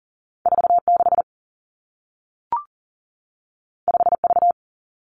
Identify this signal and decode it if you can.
Morse code, heard as a keyed tone, about 38 wpm, 700 Hz. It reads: 4 6 5 V
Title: Two Challenging Letter-Number Combinations 40wpm